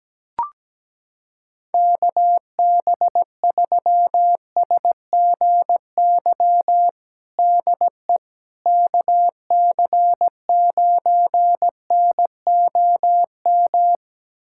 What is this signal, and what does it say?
Morse code, heard as a keyed tone, about 17 wpm, 700 Hz. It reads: KB3SGY DE KC9NOM
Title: USA Callsign DE Callsign - 6 Characters - 17wpm